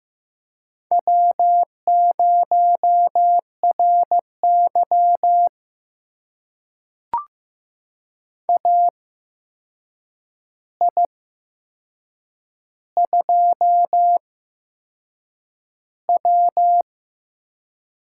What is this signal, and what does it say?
Morse code, heard as a keyed tone, about 15 wpm, spaced as for 6 wpm, 700 Hz. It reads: W0RY AI2W
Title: USA Call Signs - 4 Characters - 8x Character Spacing - 15wpm